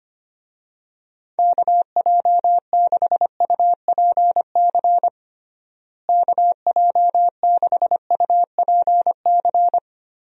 Morse code, heard as a keyed tone, about 25 wpm, 700 Hz. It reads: KJ6UPC KJ6UPC
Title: USA Call Signs - 6 Characters - Repeated 2x - 25wpm